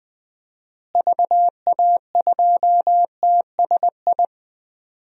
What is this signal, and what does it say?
Morse code, heard as a keyed tone, about 20 wpm, 700 Hz. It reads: VA2TSI
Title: DXCC-100 Call Signs - 6 Characters - Repeated 2x - 20wpm